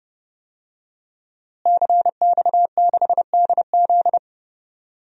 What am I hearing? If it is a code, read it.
Morse code, heard as a keyed tone, about 30 wpm, 700 Hz. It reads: CX6DZ
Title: DXCC-100 Call Signs - 5 Characters - Repeated 3x - 30wpm